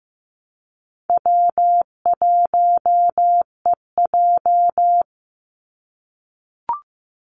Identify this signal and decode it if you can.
Morse code, heard as a keyed tone, about 15 wpm, 700 Hz. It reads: W1EJ
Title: NA SOTA Activator Callsigns 15wpm